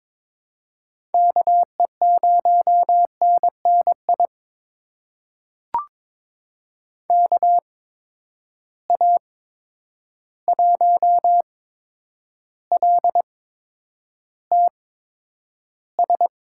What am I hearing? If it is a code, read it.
Morse code, heard as a keyed tone, about 22 wpm, spaced as for 9 wpm, 700 Hz. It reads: KE0NNI KA1LTS
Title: USA Call Signs - 6 Characters - 8x Character Spacing - 22wpm